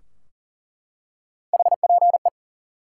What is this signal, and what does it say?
Morse code, heard as a keyed tone, about 40 wpm, 700 Hz. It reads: HPE
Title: Top 90 QSO Elements 40wpm